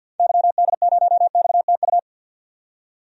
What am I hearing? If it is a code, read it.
Morse code, heard as a keyed tone, about 48 wpm, 700 Hz. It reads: YD0XTU
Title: DXCC-100 Call Signs - 6 Characters - 50wpm